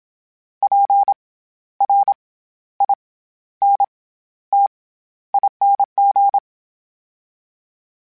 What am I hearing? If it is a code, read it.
Morse code, heard as a keyed tone, about 27 wpm, 800 Hz. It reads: P R I N T ING
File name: Special INGs-mp3s / Special INGs-mp3s / SlowToFaster ing @25